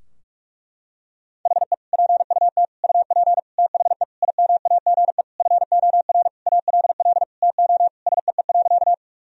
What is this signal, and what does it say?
Morse code, heard as a keyed tone, about 46 wpm, 700 Hz. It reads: HE PUT UP THE IMAGE FOR ALL TO SEE.
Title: Sentences from Top 1000 Words - Review Part 4 - 45wpm